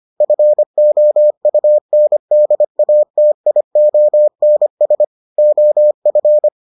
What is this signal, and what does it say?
Morse code, heard as a keyed tone, about 25 wpm, 600 Hz. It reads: FOUNDATIONS OF